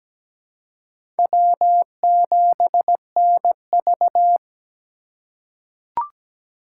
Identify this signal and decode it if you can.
Morse code, heard as a keyed tone, about 17 wpm, 700 Hz. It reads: W7NV
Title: USA Call Signs - 4 Characters - 3x Character Spacing - 17wpm